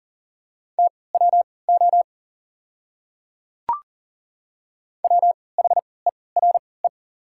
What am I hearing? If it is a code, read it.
Morse code, heard as a keyed tone, about 40 wpm, spaced as for 23 wpm, 690 Hz. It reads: TWO WHERE
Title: 3x Character Spacing - Top 200 Words - 40wpm